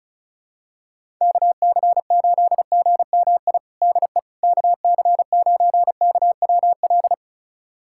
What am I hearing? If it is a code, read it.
Morse code, heard as a keyed tone, about 36 wpm, 700 Hz. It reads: KC8GMI DE KC9KWL
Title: USA Callsign DE Callsign - 6 Characters - 35wpm